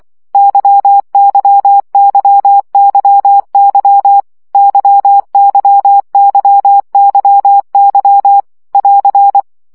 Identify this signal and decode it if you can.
Morse code, heard as a keyed tone, about 24 wpm, 800 Hz. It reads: YYYYY YYYYY <AR>
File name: CW gyakorlas y 20 baud